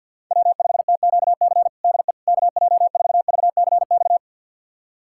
Morse code, heard as a keyed tone, about 52 wpm, 700 Hz. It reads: W5TQC DE KJ4VCX
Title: USA Callsign DE Callsign 50wpm